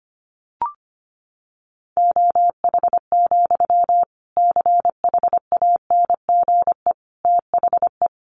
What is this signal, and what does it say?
Morse code, heard as a keyed tone, about 25 wpm, 700 Hz. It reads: OH, CHANGE THE
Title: Sentences from Top 800 Words 25wpm